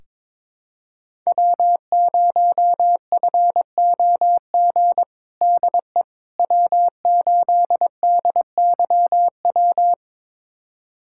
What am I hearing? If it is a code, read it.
Morse code, heard as a keyed tone, about 22 wpm, 700 Hz. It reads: W0FOG DE W8DYW
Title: USA Callsign DE Callsign - 5 Characters - 22wpm